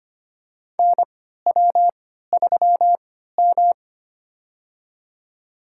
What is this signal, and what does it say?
Morse code, heard as a keyed tone, about 25 wpm, spaced as for 14 wpm, 700 Hz. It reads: NW3M
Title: USA Call Signs - 3x Character Spacing - 25wpm